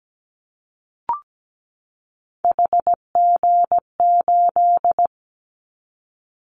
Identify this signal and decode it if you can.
Morse code, heard as a keyed tone, about 17 wpm, 700 Hz. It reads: HG8
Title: Callsign Prefixes for Top 50 DXCC - 17wpm